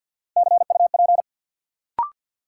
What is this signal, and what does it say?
Morse code, heard as a keyed tone, about 52 wpm, 700 Hz. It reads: CUP
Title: Sentences from Top 1600 Words - New Word Drill - 50wpm